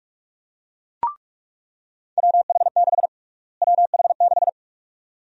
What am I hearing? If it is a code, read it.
Morse code, heard as a keyed tone, about 42 wpm, 700 Hz. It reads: WH6 WH6
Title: USA Call Sign Prefixes - Repeated 2x - 45wpm